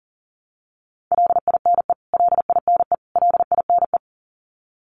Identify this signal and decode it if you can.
Morse code, heard as a keyed tone, about 40 wpm, 700 Hz. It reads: LINE LINE LINE
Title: Top 300 Words Repeated 40wpm